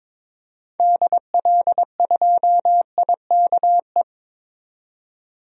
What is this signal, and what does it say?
Morse code, heard as a keyed tone, about 22 wpm, 700 Hz. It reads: DL2IKE
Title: DXCC-100 Call Signs - 6 Characters - 22wpm